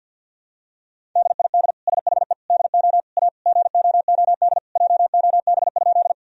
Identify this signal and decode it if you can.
Morse code, heard as a keyed tone, about 48 wpm, 700 Hz. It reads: DID SHE DO A GOOD JOB?
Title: Sentences from Top 1000 Words - Review Part 3 - 50wpm